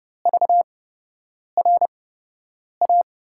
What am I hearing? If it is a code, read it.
Morse code, heard as a keyed tone, about 30 wpm, spaced as for 8 wpm, 700 Hz. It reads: VRA
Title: USA Call Signs - 6 Characters - 8x Character Spacing - 30wpm